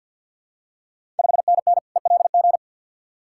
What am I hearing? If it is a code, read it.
Morse code, heard as a keyed tone, about 50 wpm, 690 Hz. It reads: 5NN ELG
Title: Ontario QSO Party Exchange 50wpm